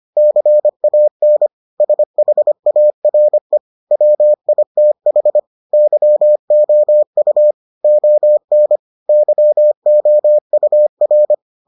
Morse code, heard as a keyed tone, about 25 wpm, 600 Hz. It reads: CAN SHARE WITH YOU ON YOUR